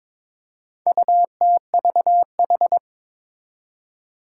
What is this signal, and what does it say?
Morse code, heard as a keyed tone, about 22 wpm, 700 Hz. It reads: UTVH
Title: Four Challenging Letter-Number Combinations 22wpm